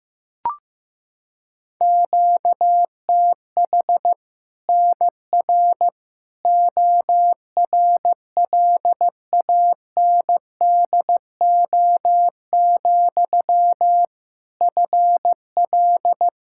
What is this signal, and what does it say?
Morse code, heard as a keyed tone, about 15 wpm, 700 Hz. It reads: QTH NR ORLANDO, FL
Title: Practice QSO - Exchange 1 - Beginning - 15wpm